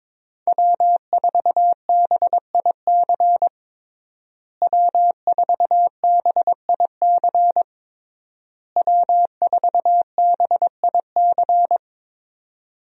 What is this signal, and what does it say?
Morse code, heard as a keyed tone, about 22 wpm, 700 Hz. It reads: W4BIC W4BIC W4BIC
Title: USA Call Signs - 5 Characters - Repeated 3x - 22wpm